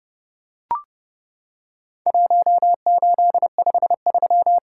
Morse code, heard as a keyed tone, about 28 wpm, 700 Hz. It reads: 1853
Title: Worked All Europe DX Contest - QTC Exchange - 30wpm